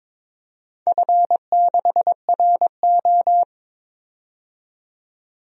Morse code, heard as a keyed tone, about 23 wpm, 700 Hz. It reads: F6RO
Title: DXCC-100 Call Signs - 4 Characters - 22wpm